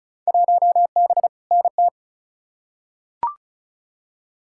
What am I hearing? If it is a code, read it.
Morse code, heard as a keyed tone, about 36 wpm, 700 Hz. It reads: 1B NT
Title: Field Day Exchanges Encore - 35wpm